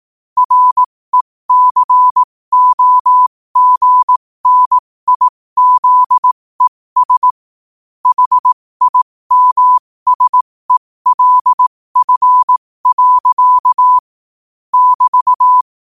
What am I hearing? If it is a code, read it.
Morse code, heard as a keyed tone, about 18 wpm, spaced as for 15 wpm, 1000 Hz. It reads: RECOGNIZES HIMSELF. <BT>
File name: qotdfiles / 15wpm